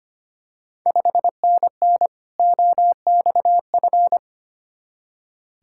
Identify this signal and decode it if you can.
Morse code, heard as a keyed tone, about 25 wpm, 700 Hz. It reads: 5NN OXF
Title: Ontario QSO Party Exchange 25wpm